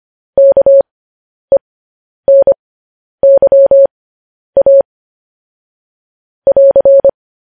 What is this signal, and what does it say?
Morse code, heard as a keyed tone, about 25 wpm, spaced as for 10 wpm, 550 Hz. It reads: KENYA <AR>
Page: News Headlines in Morse Code at 10 WPM